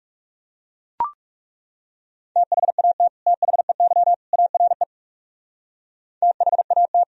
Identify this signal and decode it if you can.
Morse code, heard as a keyed tone, about 37 wpm, 700 Hz. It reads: THAT THEY ARE THAT
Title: Speed-Racing - Sets of 3 Words 30wpm